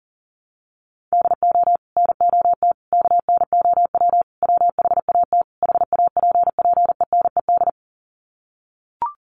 Morse code, heard as a keyed tone, about 40 wpm, 700 Hz. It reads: DO NOT KNOW WHAT HAPPENED
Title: Sets of 5 Words Encore2 40wpm